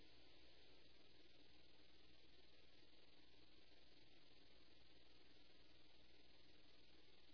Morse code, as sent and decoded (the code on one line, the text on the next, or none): none